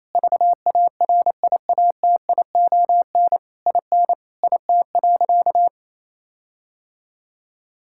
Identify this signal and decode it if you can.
Morse code, heard as a keyed tone, about 28 wpm, 700 Hz. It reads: VARIATION IN IT.
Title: Sentences from Top 1600 Words 28wpm